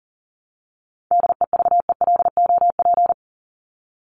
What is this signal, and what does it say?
Morse code, heard as a keyed tone, about 40 wpm, 700 Hz. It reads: DEVELOP
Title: ED Suffix - Top 100 Words with Base Words 40wpm